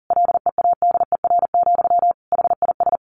Morse code, heard as a keyed tone, about 40 wpm, 700 Hz. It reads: LEADER, HIS
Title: A Sound Of Thunder 40wpm